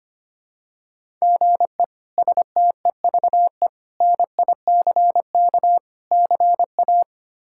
Morse code, heard as a keyed tone, about 25 wpm, 700 Hz. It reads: GE STEVE NICK CA
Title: K1USN SST QSO Practice - 25wpm